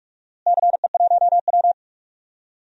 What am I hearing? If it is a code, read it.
Morse code, heard as a keyed tone, about 46 wpm, 700 Hz. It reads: CE1W
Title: DXCC-100 Call Signs - 4 Characters - Repeated 2x - 45wpm